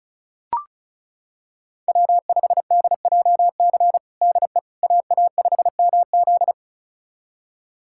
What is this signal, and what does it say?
Morse code, heard as a keyed tone, about 36 wpm, 700 Hz. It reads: W5DJC DE AA5MZ
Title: USA Callsign DE Callsign - 5 Characters - 35wpm